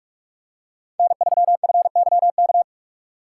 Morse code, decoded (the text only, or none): N3FYK